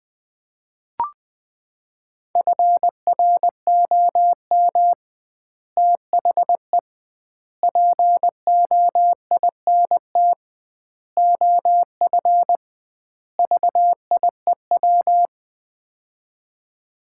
Morse code, heard as a keyed tone, about 20 wpm, 700 Hz. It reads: FROM THE POINT OF VIEW
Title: 2x Word-Spacing - Sets of 5 Words - 20wpm